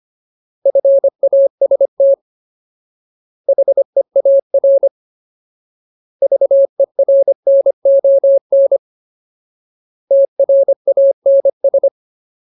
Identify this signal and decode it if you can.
Morse code, heard as a keyed tone, about 25 wpm, 550 Hz. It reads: FAST HEAR VERNON TRANS
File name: Ultimate Word List / 25WPM Ultimate